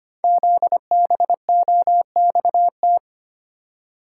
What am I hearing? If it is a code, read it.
Morse code, heard as a keyed tone, about 25 wpm, 700 Hz. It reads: ZBOXT
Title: Code Groups - 5 Letters - 25wpm